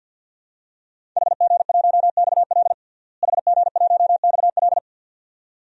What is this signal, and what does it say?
Morse code, heard as a keyed tone, about 53 wpm, 700 Hz. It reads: HG1XL HG1XL
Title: DXCC-100 Call Signs - 5 Characters - Repeated 2x - 50wpm